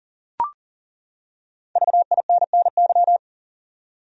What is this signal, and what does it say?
Morse code, heard as a keyed tone, about 40 wpm, 700 Hz. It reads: VINNY
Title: USA Common Names - 40wpm